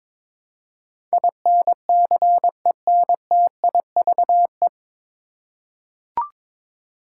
Morse code, heard as a keyed tone, about 22 wpm, 700 Hz. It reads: INCENTIVE
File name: 9 Letter Words - First 3 Letters Given - 22wpm